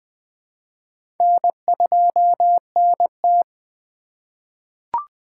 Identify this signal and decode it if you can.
Morse code, heard as a keyed tone, about 20 wpm, 700 Hz. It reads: N2NT
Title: USA Call Signs - 4x Character Spacing - 20wpm